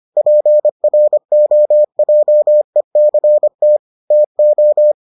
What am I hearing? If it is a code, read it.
Morse code, heard as a keyed tone, about 25 wpm, 600 Hz. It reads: PROJECT TO